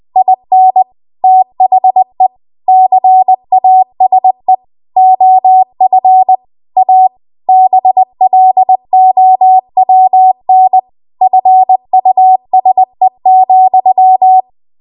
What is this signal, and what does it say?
Morse code, heard as a keyed tone, about 20 wpm, 750 Hz. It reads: IN THE CASE OF A BLOWN FUSE,
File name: Morse / 20WPM